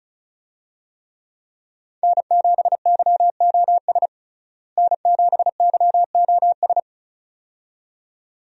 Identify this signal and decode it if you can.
Morse code, heard as a keyed tone, about 34 wpm, 700 Hz. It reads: N7YOS N7YOS
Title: USA Call Signs - 5 Characters - Repeated 2x - 35wpm